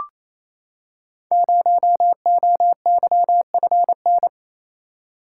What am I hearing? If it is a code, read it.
Morse code, heard as a keyed tone, about 29 wpm, 700 Hz. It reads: 0OYFN
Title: Code Groups - 5 Letter:Numbers - 28wpm